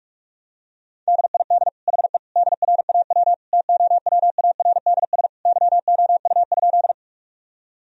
Morse code, heard as a keyed tone, about 46 wpm, 700 Hz. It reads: DID HE DRAW TOWARDS YOU?
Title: Sentences from Top 1000 Words - Review Part 2 - 45wpm